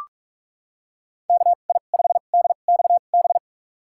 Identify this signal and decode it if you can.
Morse code, heard as a keyed tone, about 46 wpm, spaced as for 32 wpm, 700 Hz. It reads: KI5DXB
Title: USA Call Signs - 2x Character Spacing - 45wpm